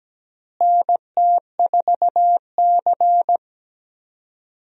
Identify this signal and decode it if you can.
Morse code, heard as a keyed tone, about 17 wpm, 700 Hz. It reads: NT4C
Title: USA Call Signs - 4 Characters - Repeated 2x - 17wpm